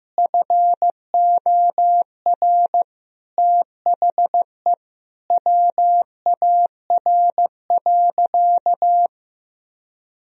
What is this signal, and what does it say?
Morse code, heard as a keyed tone, about 15 wpm, 700 Hz. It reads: FOR THE WAR.
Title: Sentences from Top 1000 Words - Review Part 4 - 15wpm